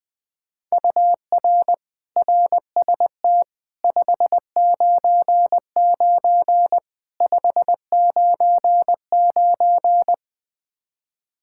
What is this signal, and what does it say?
Morse code, heard as a keyed tone, about 21 wpm, 700 Hz. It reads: UR RST 599 599
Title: Practice QSO - Exchange 1 - Beginning - 20wpm